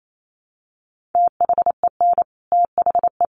Morse code, heard as a keyed tone, about 28 wpm, 700 Hz. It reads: THEN THE
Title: Sentences from Top 1100 Words 28wpm